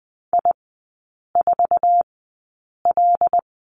Morse code, heard as a keyed tone, about 20 wpm, spaced as for 8 wpm, 700 Hz. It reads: I4L
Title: 074 Lesson 13e - Letter L - Three Characters 20wpm